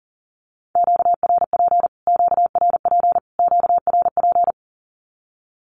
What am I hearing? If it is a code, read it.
Morse code, heard as a keyed tone, about 40 wpm, 700 Hz. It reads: QRP QRP QRP
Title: Q-Codes Repeated 3x 40wpm